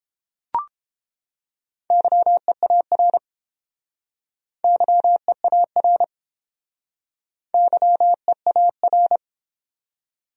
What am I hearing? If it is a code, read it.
Morse code, heard as a keyed tone, about 30 wpm, 700 Hz. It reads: YEAR YEAR YEAR